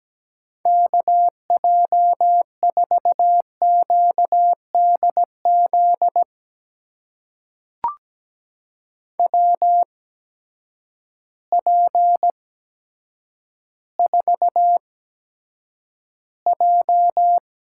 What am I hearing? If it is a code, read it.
Morse code, heard as a keyed tone, about 17 wpm, spaced as for 8 wpm, 700 Hz. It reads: KJ4QDZ WP4J
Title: USA Call Signs - 6 Characters - 8x Character Spacing - 17wpm